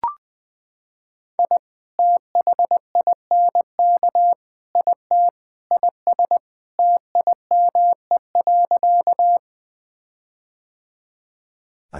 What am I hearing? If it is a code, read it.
Morse code, heard as a keyed tone, about 20 wpm, 700 Hz. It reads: I THINK IT IS TIME.
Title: Sentences with 4-Word Phrases - 20wpm